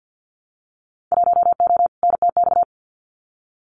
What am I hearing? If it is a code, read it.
Morse code, heard as a keyed tone, about 50 wpm, 700 Hz. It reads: 1O NTX
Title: Winter Field Day Exchanges 50wpm